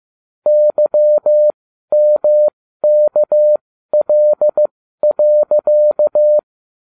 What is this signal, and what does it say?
Morse code, heard as a keyed tone, about 15 wpm, spaced as for 12 wpm, 600 Hz. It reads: YMKL.